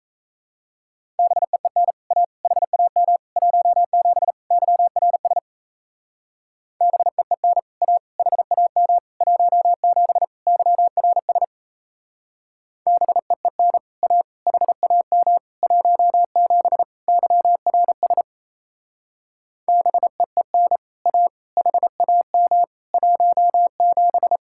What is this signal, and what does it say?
Morse code, heard as a keyed tone, about 34 wpm, 700 Hz. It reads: BEEN A HAM 17 YRS BEEN A HAM 17 YRS BEEN A HAM 17 YRS BEEN A HAM 17
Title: Speed-Racing Common QSO Phrases - 28wpm